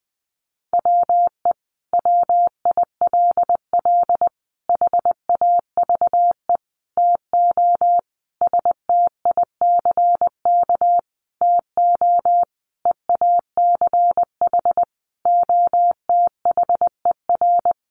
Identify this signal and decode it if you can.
Morse code, heard as a keyed tone, about 20 wpm, 700 Hz. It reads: WE WILL HAVE TO STICK TO EACH OTHER
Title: Sentences from Top 1200 Words 20wpm